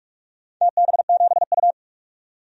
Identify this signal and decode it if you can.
Morse code, heard as a keyed tone, about 46 wpm, 700 Hz. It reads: TB7U
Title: DXCC-100 Call Signs - 4 Characters - 45wpm